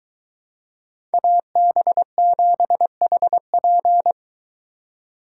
Morse code, heard as a keyed tone, about 21 wpm, 700 Hz. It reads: AB7HP